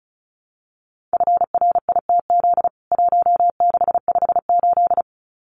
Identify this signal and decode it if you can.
Morse code, heard as a keyed tone, about 34 wpm, 700 Hz. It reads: FRITZ 1658
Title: CWT Exchange 35wpm